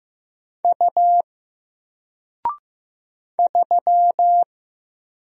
Morse code, heard as a keyed tone, about 15 wpm, 700 Hz. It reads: U 3
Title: Single Letter-Number 15wpm